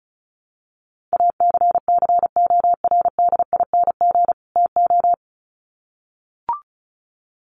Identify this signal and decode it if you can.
Morse code, heard as a keyed tone, about 34 wpm, 700 Hz. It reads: ACCORDING TO
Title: Sets of 2 Words 35wpm